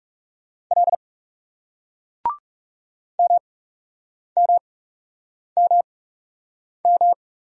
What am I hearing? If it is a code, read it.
Morse code, heard as a keyed tone, about 39 wpm, spaced as for 8 wpm, 700 Hz. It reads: L MMMM